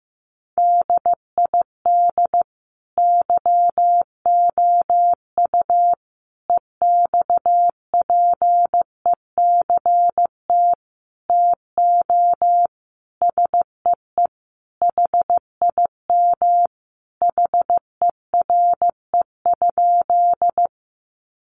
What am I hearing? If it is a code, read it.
Morse code, heard as a keyed tone, about 15 wpm, 700 Hz. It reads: DID YOU EXPECT TO SEE HIM HERE?
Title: Sentences from Top 1100 Words 15wpm